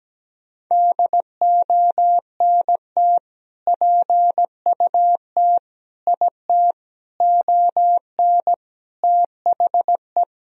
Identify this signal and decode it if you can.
Morse code, heard as a keyed tone, about 17 wpm, 700 Hz. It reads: DONT PUT IT ON THE
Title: Sentences from Top 1000 Words - Review Part 2 - 17wpm